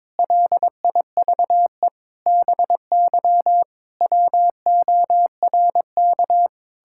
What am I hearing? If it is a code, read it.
Morse code, heard as a keyed tone, about 22 wpm, 700 Hz. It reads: LIVE BY WORK
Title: Sentences from Top 1600 Words 22wpm